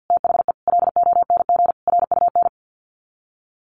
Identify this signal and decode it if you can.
Morse code, heard as a keyed tone, about 48 wpm, 700 Hz. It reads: THE LONG RUN
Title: Sets of 4 Words 50wpm